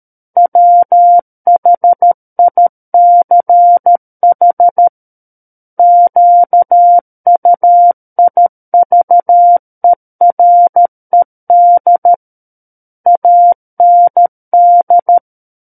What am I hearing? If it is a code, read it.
Morse code, heard as a keyed tone, about 13 wpm, 700 Hz. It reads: WHICH QUIVERED AND
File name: Book One / 04-Chapter 4 - 13 WPM